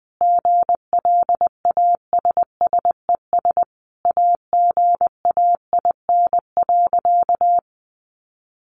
Morse code, heard as a keyed tone, about 20 wpm, 700 Hz. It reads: GLASSES AGAIN.
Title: Additional Easy Phrases 20wpm